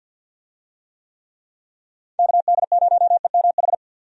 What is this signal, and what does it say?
Morse code, heard as a keyed tone, about 49 wpm, 700 Hz. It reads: KD0EMH
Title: USA Call Signs - 6 Characters - 50wpm